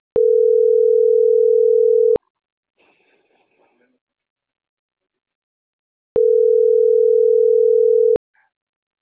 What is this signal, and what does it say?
Ringback tone (North America), 440 + 480 Hz.
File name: CENTRAL MABE / out vanesa.mabe